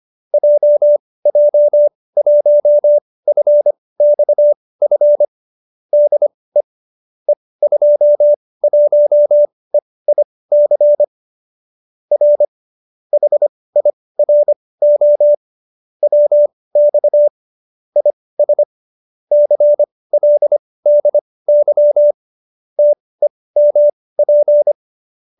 Morse code, heard as a keyed tone, about 25 wpm, spaced as for 18 wpm, 590 Hz. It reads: JJ1FXF DE E21EIC R HIRO WX IS CLDY TEMP
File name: qso208b 18